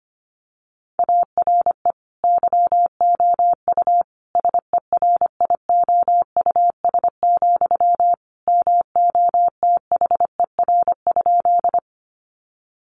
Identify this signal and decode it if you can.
Morse code, heard as a keyed tone, about 25 wpm, 700 Hz. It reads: ARE YOU SERIOUS, MOTHER?
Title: Sentences from Top 1100 Words 25wpm